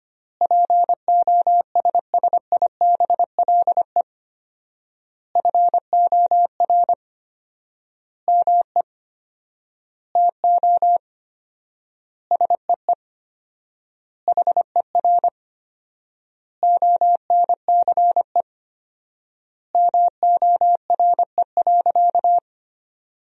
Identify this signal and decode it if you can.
Morse code, heard as a keyed tone, about 25 wpm, 700 Hz. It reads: POSSIBLE FOR ME TO SEE HER ONCE MORE.
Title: 4x Word-Spacing - Sentences from Top 500 Words - 25wpm